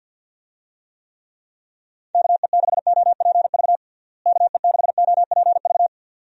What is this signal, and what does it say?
Morse code, heard as a keyed tone, about 50 wpm, 700 Hz. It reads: KE6OPV KE6OPV
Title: USA Call Signs - 6 Characters - Repeated 2x - 50wpm